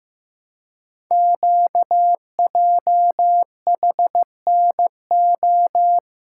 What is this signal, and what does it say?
Morse code, heard as a keyed tone, about 15 wpm, 700 Hz. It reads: QJHNO
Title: Code Groups - 5 Letters - 15wpm